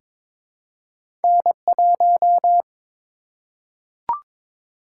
Morse code, heard as a keyed tone, about 22 wpm, 700 Hz. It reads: N1
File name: Two Letter-Number Combinations - 22wpm